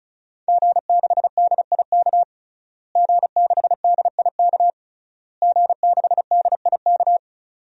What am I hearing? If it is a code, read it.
Morse code, heard as a keyed tone, about 36 wpm, 700 Hz. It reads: G6DIK G6DIK G6DIK
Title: DXCC-100 Call Signs - 5 Characters - Repeated 3x - 35wpm